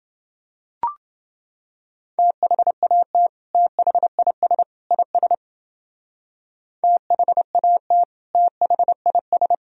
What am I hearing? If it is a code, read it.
Morse code, heard as a keyed tone, about 29 wpm, 700 Hz. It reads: THAT THIS IS THAT THIS